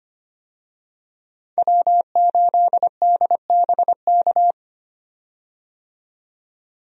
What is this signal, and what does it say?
Morse code, heard as a keyed tone, about 25 wpm, 700 Hz. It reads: W8DBK
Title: USA Call Signs - 5 Characters - 25wpm